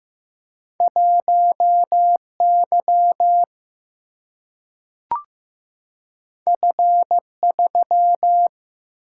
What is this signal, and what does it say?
Morse code, heard as a keyed tone, about 15 wpm, 700 Hz. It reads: 1 Y F 3
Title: Single-Letter Single-Number 15wpm